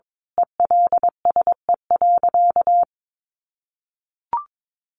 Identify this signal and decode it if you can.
Morse code, heard as a keyed tone, about 22 wpm, 700 Hz. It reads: ELSE.
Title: Sentences from Top 1500 Words 22wpm